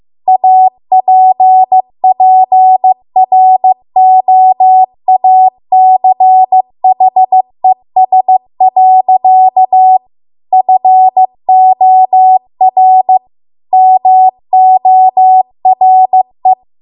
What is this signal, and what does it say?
Morse code, heard as a keyed tone, about 15 wpm, 750 Hz. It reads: APPROACHES. FOR MORE